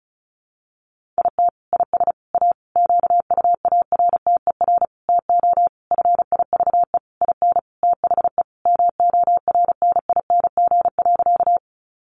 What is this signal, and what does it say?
Morse code, heard as a keyed tone, about 34 wpm, 700 Hz. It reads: IT IS A QUARTER TO FIVE IN THE MORNING.
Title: Sentences from Top 1300 Words 35wpm